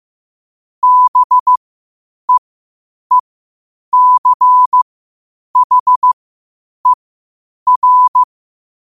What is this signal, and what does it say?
Morse code, heard as a keyed tone, about 15 wpm, spaced as for 8 wpm, 1000 Hz. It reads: BEECHER